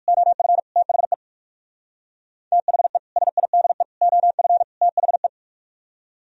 Morse code, heard as a keyed tone, about 59 wpm, 700 Hz. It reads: OF THE THE SIDE OF THE